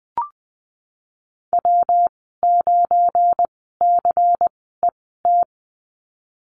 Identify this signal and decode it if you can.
Morse code, heard as a keyed tone, about 20 wpm, spaced as for 14 wpm, 700 Hz. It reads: W9CET
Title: 173 Lesson 28g - Number 9 - Call Signs 20wpm